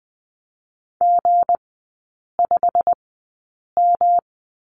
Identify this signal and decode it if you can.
Morse code, heard as a keyed tone, about 20 wpm, spaced as for 8 wpm, 690 Hz. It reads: G5M
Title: 212 Lesson 34e - Number 8 - Three Characters 20wpm